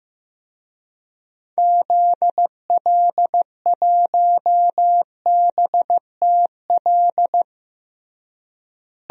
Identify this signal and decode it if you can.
Morse code, heard as a keyed tone, about 15 wpm, 700 Hz. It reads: ZL1BTL
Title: DXCC-100 Call Signs - 6 Characters - Repeated 3x - 15wpm